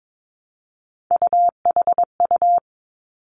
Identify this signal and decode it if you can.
Morse code, heard as a keyed tone, about 22 wpm, 700 Hz. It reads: UHU
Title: Three Challenging Letter-Number Combinations 22wpm